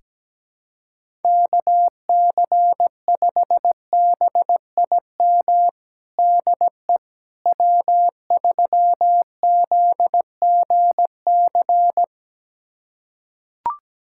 Morse code, heard as a keyed tone, about 17 wpm, 700 Hz. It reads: KC5BIM DE W3ZGC
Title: USA Callsign DE Callsign 17wpm